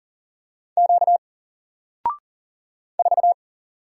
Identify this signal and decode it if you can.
Morse code, heard as a keyed tone, about 40 wpm, 700 Hz. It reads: Q 4
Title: Single Character (Letter:Number:Punctuation:Prosign) Alt 40wpm